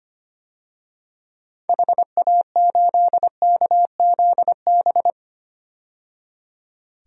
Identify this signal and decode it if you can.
Morse code, heard as a keyed tone, about 25 wpm, 700 Hz. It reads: HA8KZB
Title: DXCC-100 Call Signs - 6 Characters - 25wpm